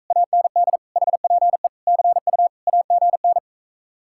Morse code, heard as a keyed tone, about 45 wpm, 700 Hz. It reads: AND HPE CU AGN